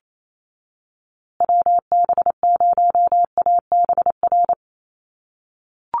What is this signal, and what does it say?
Morse code, heard as a keyed tone, about 28 wpm, 700 Hz. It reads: WB0ABR
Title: USA Call Signs Repeated 3x 28wpm